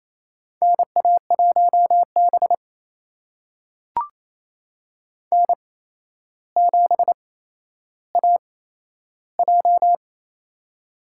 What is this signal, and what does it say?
Morse code, heard as a keyed tone, about 28 wpm, spaced as for 12 wpm, 700 Hz. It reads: NA1B N7AJ
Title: USA Call Signs - 4 Characters - 8x Character Spacing - 28wpm